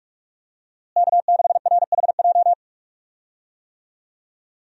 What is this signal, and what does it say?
Morse code, heard as a keyed tone, about 41 wpm, 700 Hz. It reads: K6RHJ
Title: USA Call Signs - 5 Characters - Repeated 2x - 45wpm